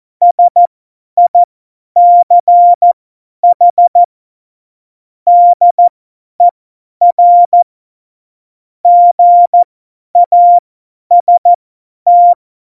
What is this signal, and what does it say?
Morse code, heard as a keyed tone, about 14 wpm, spaced as for 10 wpm, 700 Hz. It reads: SICH DER GAST
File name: Morsekurs / Witze / Witz 03.Dezember 2019, 50 F70